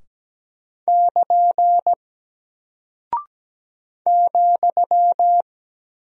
Morse code, heard as a keyed tone, about 17 wpm, 700 Hz. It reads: <KN> ,
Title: Single Character (Letter:Number:Punctuation:Prosign) Alt 17wpm